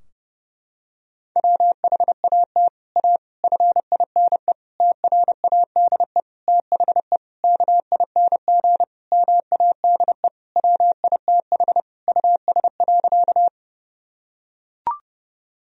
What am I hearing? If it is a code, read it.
Morse code, heard as a keyed tone, about 32 wpm, 700 Hz. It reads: WHAT A FINE TRADE THE KING MADE WITH US.
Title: Sentences from Top 1000 Words - Review Part 4 - 30wpm